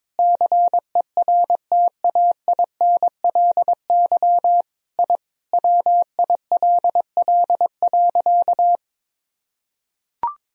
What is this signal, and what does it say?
Morse code, heard as a keyed tone, about 22 wpm, 700 Hz. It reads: CERTAINLY I WILL.
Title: Sentences from Top 1000 Words - Review Part 1 - 22wpm